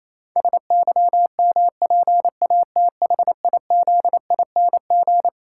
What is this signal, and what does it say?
Morse code, heard as a keyed tone, about 28 wpm, 700 Hz. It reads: SYMPATHIZING
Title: Pangram Challenge v2 - 28wpm